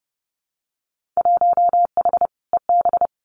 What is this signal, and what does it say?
Morse code, heard as a keyed tone, about 30 wpm, 700 Hz. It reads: 1H EB
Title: Winter Field Day Exchanges 30wpm